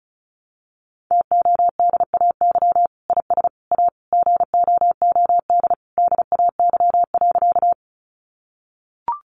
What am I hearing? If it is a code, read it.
Morse code, heard as a keyed tone, about 34 wpm, 700 Hz. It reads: TODAY IS A GOOD DAY.
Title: Sentences from Top 600 Words 35wpm